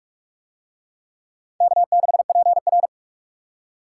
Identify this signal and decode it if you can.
Morse code, heard as a keyed tone, about 42 wpm, 700 Hz. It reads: K6PR